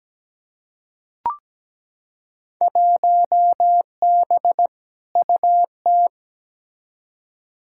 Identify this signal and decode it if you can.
Morse code, heard as a keyed tone, about 17 wpm, 700 Hz. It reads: 1B UT
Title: Field Day Exchanges - 17wpm